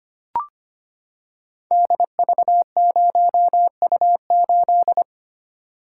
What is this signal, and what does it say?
Morse code, heard as a keyed tone, about 25 wpm, 700 Hz. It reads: DV0U8
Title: Code Groups - 5 Letter:Numbers - 25wpm